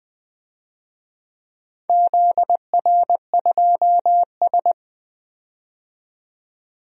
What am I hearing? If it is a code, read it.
Morse code, heard as a keyed tone, about 20 wpm, 700 Hz. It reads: ZR2S
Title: DXCC-100 Call Signs - 4 Characters - 20wpm